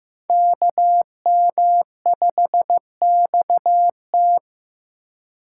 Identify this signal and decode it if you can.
Morse code, heard as a keyed tone, about 18 wpm, 700 Hz. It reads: KM5XT
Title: Speed Racing - USA 2x2 Call Signs 15wpm